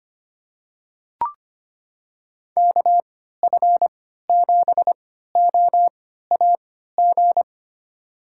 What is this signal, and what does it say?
Morse code, heard as a keyed tone, about 25 wpm, spaced as for 14 wpm, 700 Hz. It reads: KF7OAG
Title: USA Call Signs - 6 Characters - 3x Character Spacing - 25wpm